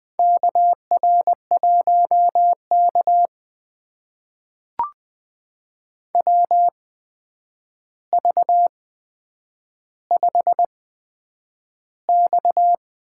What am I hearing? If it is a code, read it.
Morse code, heard as a keyed tone, about 21 wpm, spaced as for 9 wpm, 700 Hz. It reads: KR1K WV5X
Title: USA Call Signs - 4 Characters - 8x Character Spacing - 20wpm